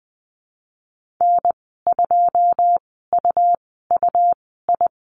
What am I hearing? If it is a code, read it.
Morse code, heard as a keyed tone, about 19 wpm, spaced as for 14 wpm, 700 Hz. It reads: N2UUI
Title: 122 Lesson 20g - Number 3 - Call Signs 20wpm